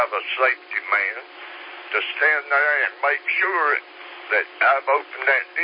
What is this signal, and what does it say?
Emergency Alert System (SAME) tones, heard as harsh buzzing bursts, with speech. It is heard from 0.7 to 4.6 s.